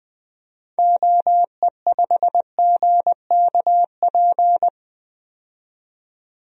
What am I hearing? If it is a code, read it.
Morse code, heard as a keyed tone, about 20 wpm, 700 Hz. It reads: OE5GKP
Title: DXCC-100 Call Signs - 6 Characters - 20wpm